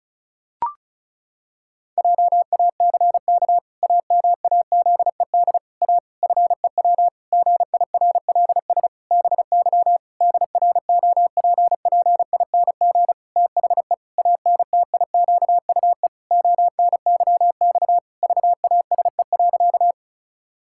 Morse code, heard as a keyed tone, about 34 wpm, 700 Hz. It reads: JACK AMAZED A FEW GIRLS BY DROPPING THE ANTIQUE ONYX VASE.
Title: Pangram Challenge v2 - 35wpm